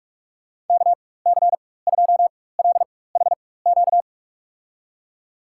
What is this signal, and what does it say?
Morse code, heard as a keyed tone, about 46 wpm, spaced as for 21 wpm, 700 Hz. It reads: KC2LHQ
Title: USA Call Signs - 6 Characters - 4x Character Spacing - 45wpm